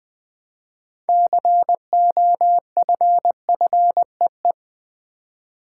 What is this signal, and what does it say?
Morse code, heard as a keyed tone, about 20 wpm, 700 Hz. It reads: COFFEE
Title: Build A Word - 2 Letters At A Time - 6-Letter Words - 20wpm